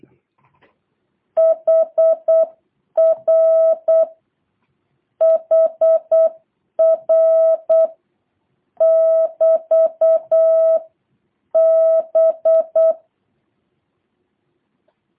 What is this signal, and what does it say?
Morse code, heard as a keyed tone, about 8 wpm, 650 Hz. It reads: HR HR <BT> B